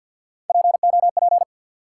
Morse code, heard as a keyed tone, about 52 wpm, 700 Hz. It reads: POP
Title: Three Letter Words - 50wpm